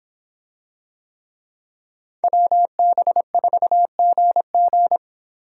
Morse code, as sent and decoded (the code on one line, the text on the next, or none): .-- -... ....- --. --.
WB4GG